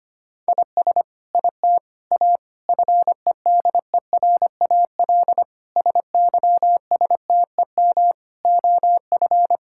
Morse code, heard as a keyed tone, about 25 wpm, 700 Hz. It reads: IS IT A FEDERAL SYSTEM OF
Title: Sentences from Top 1000 Words - Review Part 5 - 25wpm